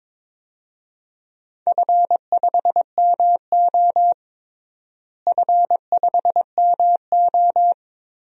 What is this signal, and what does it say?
Morse code, heard as a keyed tone, about 22 wpm, 700 Hz. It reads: F5MO F5MO
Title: DXCC-100 Call Signs - 4 Characters - Repeated 2x - 22wpm